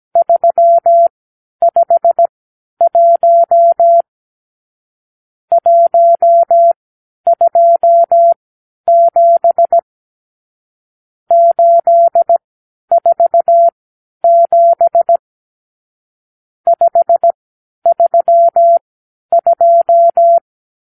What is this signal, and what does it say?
Morse code, heard as a keyed tone, about 17 wpm, spaced as for 10 wpm, 680 Hz. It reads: 351 127 847 532
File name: k7qo / Lesson 54 (9)